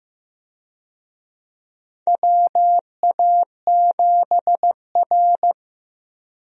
Morse code, heard as a keyed tone, about 15 wpm, 700 Hz. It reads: WA7R
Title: USA Call Signs - 4 Characters - Repeated 2x - 15wpm